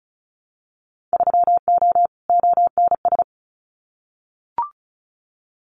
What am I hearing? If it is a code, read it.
Morse code, heard as a keyed tone, about 36 wpm, 700 Hz. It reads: 3O ONS
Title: Winter Field Day Exchanges 35wpm